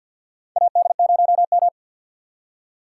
Morse code, heard as a keyed tone, about 48 wpm, 700 Hz. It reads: AD0M
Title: USA Call Signs - 4 Characters - 2x Character Spacing - 50wpm